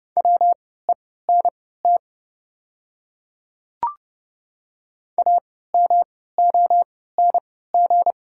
Morse code, heard as a keyed tone, about 29 wpm, spaced as for 17 wpm, 700 Hz. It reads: WENT AMONG
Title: 3x Character Spacing - Top 500 Words - 30wpm